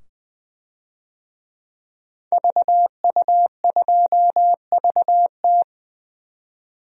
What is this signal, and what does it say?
Morse code, heard as a keyed tone, about 20 wpm, 700 Hz. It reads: VU2VT
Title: DXCC-100 Call Signs - 5 Characters - 20wpm